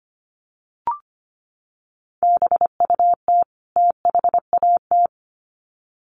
Morse code, heard as a keyed tone, about 25 wpm, 700 Hz. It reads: BUT THAT
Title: Sets of 2 Words 25wpm